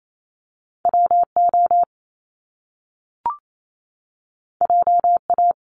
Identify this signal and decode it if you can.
Morse code, heard as a keyed tone, about 28 wpm, 700 Hz. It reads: WO JA
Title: Top Two Letter Combinations in Words 28wpm